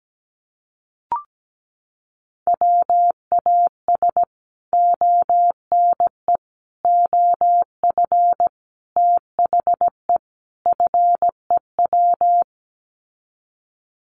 Morse code, heard as a keyed tone, about 17 wpm, 700 Hz. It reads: WAS ONE OF THE FEW
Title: Sets of 5 Words Encore 17wpm